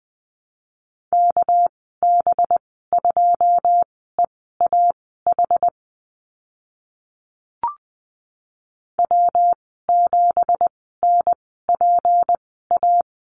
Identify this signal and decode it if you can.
Morse code, heard as a keyed tone, about 20 wpm, spaced as for 14 wpm, 700 Hz. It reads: KB2EAH W7NPA
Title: 245 Lesson 39g - Letter z - Call Signs 20wpm